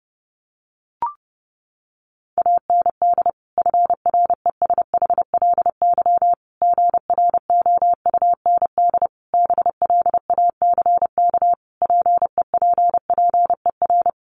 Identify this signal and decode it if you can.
Morse code, heard as a keyed tone, about 30 wpm, 700 Hz. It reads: AND FRESHLY GROUND BLACK PEPPER
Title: Sets of 5 Words Encore 30wpm